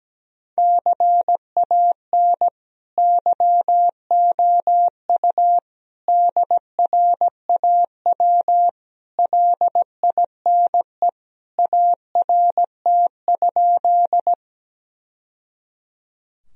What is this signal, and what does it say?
Morse code, heard as a keyed tone, about 17 wpm, 700 Hz. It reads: CAN YOU DRAW LINE ART?
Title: Sentences from Top 1000 Words - Review Part 3 - 17wpm